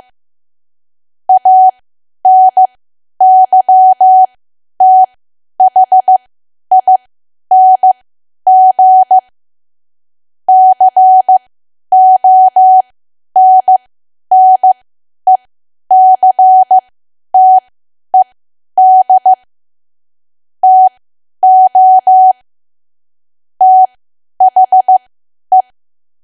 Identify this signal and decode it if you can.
Morse code, heard as a keyed tone, about 15 wpm, spaced as for 10 wpm, 750 Hz. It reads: ANYTHING CONNECTED TO THE
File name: Archive / 10 WPM / 10WPM